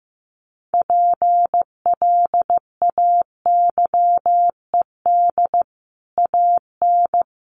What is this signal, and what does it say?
Morse code, heard as a keyed tone, about 15 wpm, 700 Hz. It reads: PLAYED AN
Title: Sets of 5 Words Encore2 15wpm